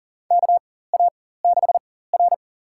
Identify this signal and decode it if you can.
Morse code, heard as a keyed tone, about 40 wpm, spaced as for 19 wpm, 700 Hz. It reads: KA6R
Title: USA Call Signs - 4 Characters - 4x Character Spacing - 40wpm